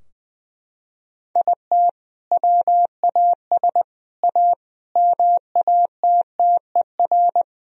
Morse code, heard as a keyed tone, about 21 wpm, 700 Hz. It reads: IT WAS A MATTER
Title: Sentences from Top 1000 Words - Review Part 4 - 20wpm